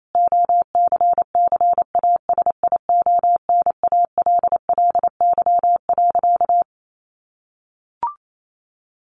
Morse code, heard as a keyed tone, about 28 wpm, 700 Hz. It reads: OCCASIONALLY.
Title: Sentences from Top 1300 Words 28wpm